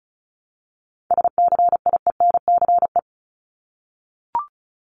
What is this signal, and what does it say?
Morse code, heard as a keyed tone, about 34 wpm, 700 Hz. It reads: SCIENCE
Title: Top 500 Words 35wpm